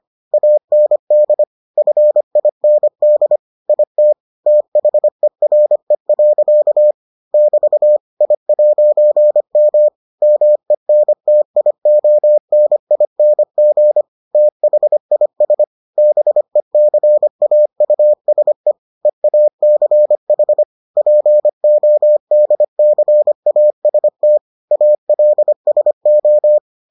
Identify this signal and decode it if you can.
Morse code, heard as a keyed tone, about 25 wpm, 600 Hz. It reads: AND FIND IT THERE. <BT> I'M MENTIONING THIS BECAUSE EACH PODCAST ALSO